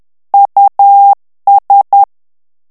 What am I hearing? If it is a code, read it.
Morse code, heard as a keyed tone, about 10 wpm, 800 Hz. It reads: US
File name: ANGLAIS texte clair 3 vitesse 10 mots minute